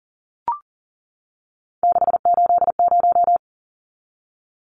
Morse code, heard as a keyed tone, about 40 wpm, 700 Hz. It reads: 680
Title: Numbers - 3-Digit 40wpm